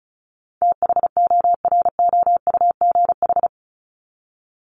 Morse code, heard as a keyed tone, about 34 wpm, 700 Hz. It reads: THOROUGH
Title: LY Suffix - Top 100 Words with Base Words 35wpm